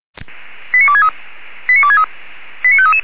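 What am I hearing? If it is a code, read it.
Five-tone selective call (ZVEI), heard as quick runs of five tones, tones of about 70 ms. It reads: 89263 89263 89533
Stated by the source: ZVEI five-tone selective call received by SDR.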